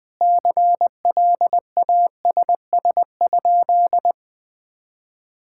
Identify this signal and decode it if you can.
Morse code, heard as a keyed tone, about 21 wpm, 700 Hz. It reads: CLASS?
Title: Sentences from Top 1000 Words - Review Part 6 - 20wpm